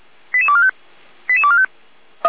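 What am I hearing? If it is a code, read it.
Five-tone selective call (ZVEI), heard as quick runs of five tones, tones of about 70 ms. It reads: 88245 88245
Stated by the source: ZVEI five-tone selective call received by SDR.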